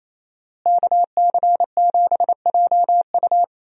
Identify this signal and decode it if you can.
Morse code, heard as a keyed tone, about 28 wpm, 700 Hz. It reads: KC7JU
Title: USA Call Signs - 5 Characters - Repeated 2x - 28wpm